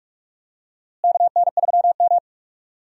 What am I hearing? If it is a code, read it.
Morse code, heard as a keyed tone, about 42 wpm, 700 Hz. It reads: KN3M
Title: USA Call Signs - 4 Characters - Repeated 2x - 45wpm